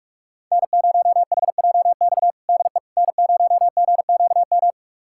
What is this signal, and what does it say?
Morse code, heard as a keyed tone, about 41 wpm, 700 Hz. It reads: N0HJX DE N0GQM
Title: USA Callsign DE Callsign - 5 Characters - 45wpm